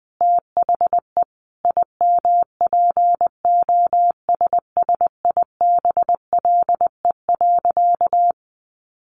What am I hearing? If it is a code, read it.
Morse code, heard as a keyed tone, about 20 wpm, 700 Hz. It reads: THE IMPOSSIBLE.
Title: Sentences from Top 1100 Words 20wpm